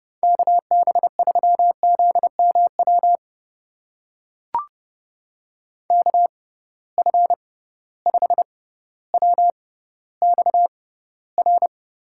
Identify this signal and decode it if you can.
Morse code, heard as a keyed tone, about 30 wpm, spaced as for 15 wpm, 700 Hz. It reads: KB3ZMW KF5WXR
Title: USA Call Signs - 6 Characters - 6x Character Spacing - 30wpm